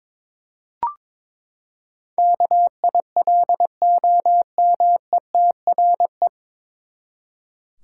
Morse code, heard as a keyed tone, about 22 wpm, 700 Hz. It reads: KILOMETRE
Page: Morse Code Ninja practice 22wpm